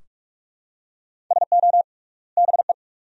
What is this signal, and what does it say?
Morse code, heard as a keyed tone, about 46 wpm, 700 Hz. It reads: SO BE
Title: 3x Word-Spacing - Sentences from Top 100 Words - 45wpm